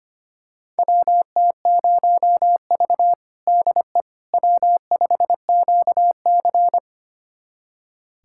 Morse code, heard as a keyed tone, about 25 wpm, 700 Hz. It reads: WT0V DE W5QC
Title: USA Callsign DE Callsign - 4 Characters - 25wpm